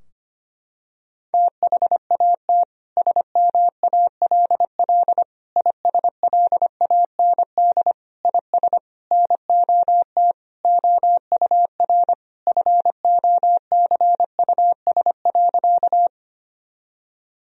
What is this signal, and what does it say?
Morse code, heard as a keyed tone, about 25 wpm, 700 Hz. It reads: THAT SMALL ISLAND IS NOT OUR FOCUS.
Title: Sentences from Top 1000 Words - Review Part 4 - 25wpm